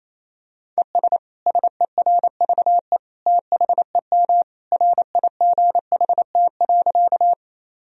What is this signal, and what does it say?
Morse code, heard as a keyed tone, about 28 wpm, 700 Hz. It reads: ES SERVE THEM RIGHT.
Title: Sentences from Top 400 Words and Abbreviations 28wpm